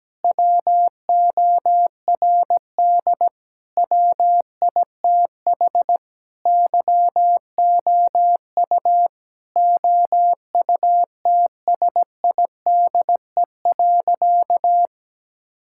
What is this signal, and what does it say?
Morse code, heard as a keyed tone, about 17 wpm, 700 Hz. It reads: WORD WITH YOU OUTSIDE.
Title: Sentences from Top 1000 Words - Review Part 6 - 17wpm